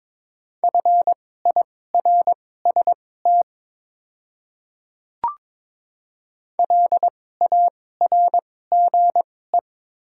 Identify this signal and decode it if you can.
Morse code, heard as a keyed tone, about 23 wpm, spaced as for 16 wpm, 700 Hz. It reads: FIRST LARGE
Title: 2x Character Spacing - Top 200 Words - 22wpm